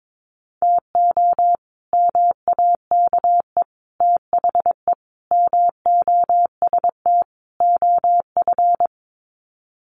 Morse code, heard as a keyed tone, about 22 wpm, 700 Hz. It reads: TO MAKE THE MOST OF
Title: Sets of 5 Words Encore2 22wpm